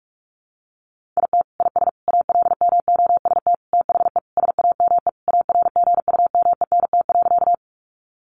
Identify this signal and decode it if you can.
Morse code, heard as a keyed tone, about 44 wpm, 700 Hz. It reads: IT IS ALMOST THE SAME ARGUMENT.
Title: Sentences from Top 1400 Words 45wpm